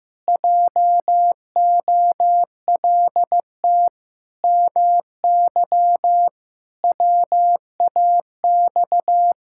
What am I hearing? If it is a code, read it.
Morse code, heard as a keyed tone, about 15 wpm, 700 Hz. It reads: JOLT MY WAX
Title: Pangram Challenge v2 - 15wpm